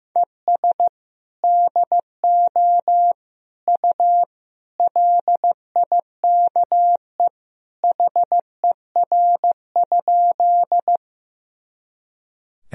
Morse code, heard as a keyed tone, about 15 wpm, 700 Hz. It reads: ES DO U LIKE HER?
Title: Sentences from Top 100 Words and Abbreviations 15wpm